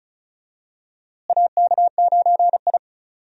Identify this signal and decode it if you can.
Morse code, heard as a keyed tone, about 34 wpm, 700 Hz. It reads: AK9I
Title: USA Call Signs - 4 Characters - Repeated 3x - 35wpm